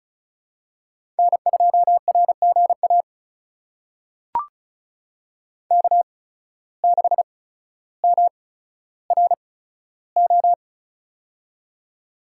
Morse code, heard as a keyed tone, about 36 wpm, spaced as for 15 wpm, 700 Hz. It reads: N2RGA K6MRO
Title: USA Call Signs - 8x Character Spacing - 35wpm